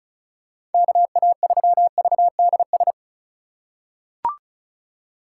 Morse code, heard as a keyed tone, about 34 wpm, 700 Hz. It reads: KA3VDS
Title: USA Call Signs - 6 Characters - 4x Character Spacing - 35wpm